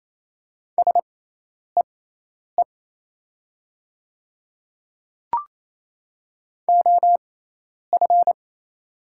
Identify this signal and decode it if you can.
Morse code, heard as a keyed tone, about 28 wpm, spaced as for 10 wpm, 700 Hz. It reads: SEE OF
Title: 6x Character Spacing - Top 100 Words - 28wpm